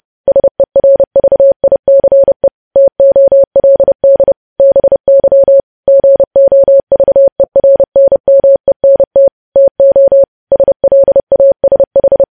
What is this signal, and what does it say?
Morse code, heard as a keyed tone, about 30 wpm, 550 Hz. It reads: SERVICE TOLD BY GOVERNMENT TO SLASH
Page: news headlines in morse code at 30 wpm